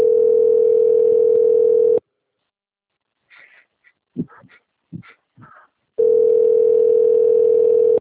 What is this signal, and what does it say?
Ringback tone (North America), 440 + 480 Hz.
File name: out- - - - - .5131